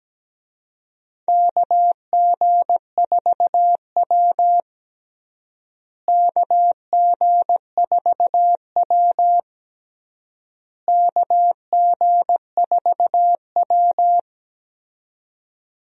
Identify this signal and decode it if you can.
Morse code, heard as a keyed tone, about 17 wpm, 700 Hz. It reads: KG4W KG4W KG4W
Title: USA Call Signs - 4 Characters - Repeated 3x - 17wpm